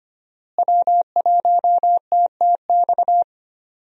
Morse code, heard as a keyed tone, about 25 wpm, 700 Hz. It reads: W1TTX
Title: USA Call Signs - 5 Characters - Repeated 2x - 25wpm